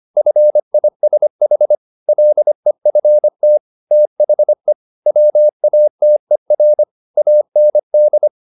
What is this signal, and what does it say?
Morse code, heard as a keyed tone, about 25 wpm, 600 Hz. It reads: FISH LEFT THE WATER AND
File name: the old man and the sea part 1 25wpm